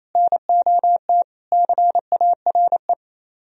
Morse code, heard as a keyed tone, about 33 wpm, 700 Hz. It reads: NOT CARE
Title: Speed-Racing - Sets of 4 Words 28wpm